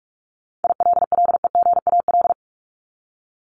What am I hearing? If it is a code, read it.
Morse code, heard as a keyed tone, about 44 wpm, 700 Hz. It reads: ILLEGAL
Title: Top 1751-2000 Words 45wpm